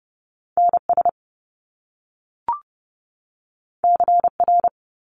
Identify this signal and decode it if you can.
Morse code, heard as a keyed tone, about 30 wpm, 700 Hz. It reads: NS CR
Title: Top Two Letter Combinations in Words 30wpm